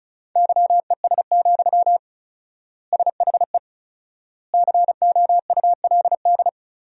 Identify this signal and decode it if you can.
Morse code, heard as a keyed tone, about 36 wpm, 700 Hz. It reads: YES, SHE COULD
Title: 4x Word-Spacing - Sentences from Top 500 Words - 35wpm